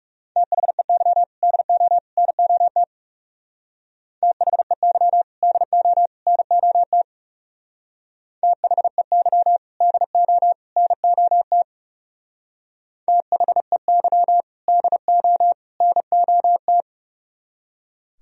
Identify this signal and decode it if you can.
Morse code, heard as a keyed tone, about 38 wpm, 700 Hz. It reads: THEY DO NOT THEY DO NOT THEY DO NOT THEY DO NOT